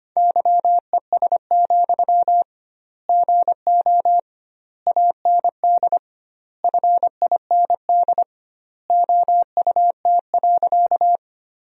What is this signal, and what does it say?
Morse code, heard as a keyed tone, about 25 wpm, 700 Hz. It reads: YES, GO AND FIND OUT.
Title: 2x Word-Spacing - Sentences from Top 500 Words - 25wpm